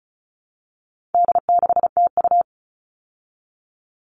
Morse code, heard as a keyed tone, about 34 wpm, 700 Hz. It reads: D6TU
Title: All Cut Numbers - 4-Digit 35wpm